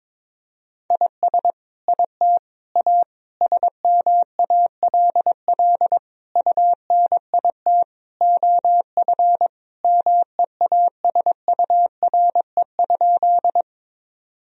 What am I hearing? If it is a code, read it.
Morse code, heard as a keyed tone, about 22 wpm, 700 Hz. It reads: IS IT A SMALL UNIT OF MEASURE?
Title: Sentences from Top 1000 Words - Review Part 2 - 22wpm